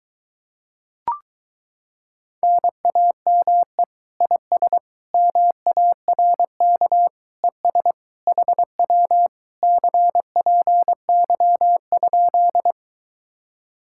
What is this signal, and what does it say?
Morse code, heard as a keyed tone, about 23 wpm, 700 Hz. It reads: NAME IS MARK ES HW CPY?